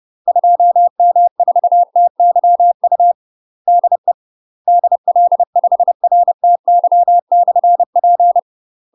Morse code, heard as a keyed tone, about 30 wpm, 700 Hz. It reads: 2M4TYU DE DL5RTY/P
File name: UK MASTER QSOs TRAINING / 30WPM / UK MASTER QSOs 0003